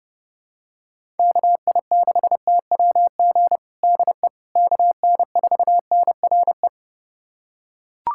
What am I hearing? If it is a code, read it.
Morse code, heard as a keyed tone, about 30 wpm, 700 Hz. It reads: KI6TWG DE KN4NRE
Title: USA Callsign DE Callsign - 6 Characters - 30wpm